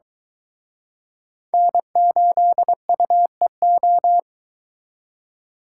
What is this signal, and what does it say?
Morse code, heard as a keyed tone, about 23 wpm, 700 Hz. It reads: N8UEO